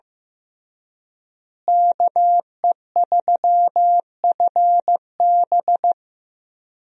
Morse code, heard as a keyed tone, about 15 wpm, 700 Hz. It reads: KE3FB
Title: USA Call Signs - 5 Characters - Repeated 3x - 15wpm